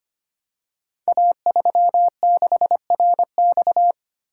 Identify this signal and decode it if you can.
Morse code, heard as a keyed tone, about 25 wpm, 700 Hz. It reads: A36RX
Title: Code Groups - 5 Letter:Numbers - 25wpm